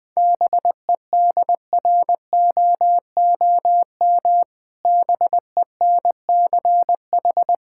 Morse code, heard as a keyed tone, about 20 wpm, 700 Hz. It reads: BEDROOM BENCH
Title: Difficult Words - Speed-Racing with Hints - 20wpm